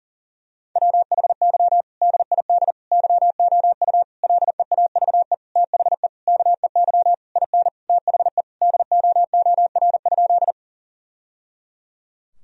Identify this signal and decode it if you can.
Morse code, heard as a keyed tone, about 40 wpm, 700 Hz. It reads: WHY DID YOU LEAVE THE KEY IN THE DOOR?
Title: Sentences from Top 1000 Words - Review Part 4 - 40wpm